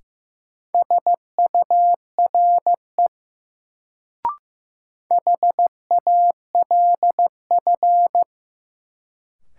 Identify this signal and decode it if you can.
Morse code, heard as a keyed tone, about 16 wpm, 700 Hz. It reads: SURE HALF
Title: Sets of 3 Words - New Word Drill - 15wpm